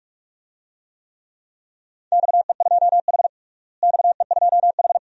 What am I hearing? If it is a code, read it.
Morse code, heard as a keyed tone, about 45 wpm, 700 Hz. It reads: XE2H XE2H
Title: DXCC-100 Call Signs - 4 Characters - Repeated 2x - 45wpm